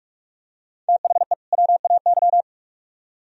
Morse code, heard as a keyed tone, about 40 wpm, 700 Hz. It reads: THE WAY